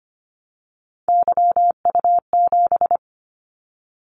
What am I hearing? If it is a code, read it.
Morse code, heard as a keyed tone, about 25 wpm, 700 Hz. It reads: YU7
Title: Callsign Prefixes for Top 25 DXCC - 25wpm